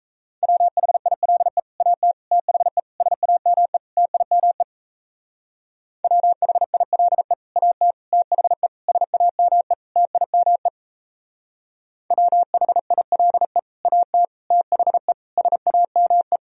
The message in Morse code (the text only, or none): WHILE AT THE SAME TIME WHILE AT THE SAME TIME WHILE AT THE SAME